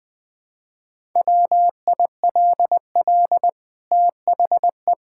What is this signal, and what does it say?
Morse code, heard as a keyed tone, about 21 wpm, 700 Hz. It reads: WILL THE
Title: Sentences from Top 1000 Words - Review Part 5 - 20wpm